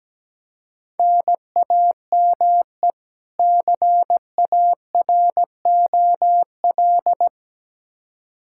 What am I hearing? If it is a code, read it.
Morse code, heard as a keyed tone, about 17 wpm, 700 Hz. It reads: NAME CAROL
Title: K1USN SST QSO Practice 17wpm